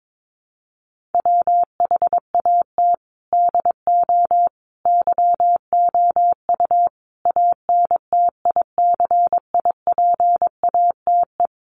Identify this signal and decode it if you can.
Morse code, heard as a keyed tone, about 22 wpm, 700 Hz. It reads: WHAT DO YOU ANTICIPATE
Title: Sentences from Top 1500 Words 22wpm